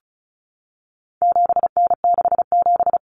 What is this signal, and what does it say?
Morse code, heard as a keyed tone, about 34 wpm, 700 Hz. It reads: 7N67
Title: Common Cut Numbers - 4-Digit 35wpm